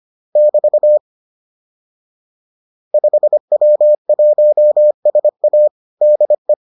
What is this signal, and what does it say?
Morse code, heard as a keyed tone, about 25 wpm, 600 Hz. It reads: <BT> 5W1SA DE
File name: 25wpm-6